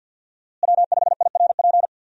Morse code, heard as a keyed tone, about 52 wpm, 700 Hz. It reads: W5IRP
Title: USA Call Signs - 5 Characters - Repeated 3x - 50wpm